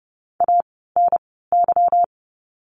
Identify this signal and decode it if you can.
Morse code, heard as a keyed tone, about 30 wpm, spaced as for 17 wpm, 700 Hz. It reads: ANY
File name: course / 30:10wpm / 230 Lesson 37f - Letter x - Words 30wpm